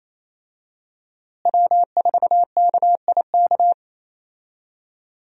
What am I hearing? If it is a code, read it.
Morse code, heard as a keyed tone, about 28 wpm, 700 Hz. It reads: W4KIK
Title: USA Call Signs - 5 Characters - Repeated 2x - 28wpm